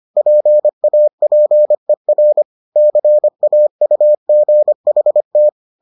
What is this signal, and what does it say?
Morse code, heard as a keyed tone, about 25 wpm, 600 Hz. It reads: PAPER CAUGHT